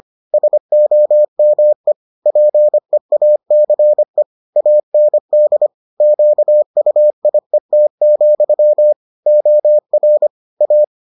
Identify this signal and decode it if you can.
Morse code, heard as a keyed tone, about 25 wpm, 600 Hz. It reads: SOME PEACE AND QUIET, OR A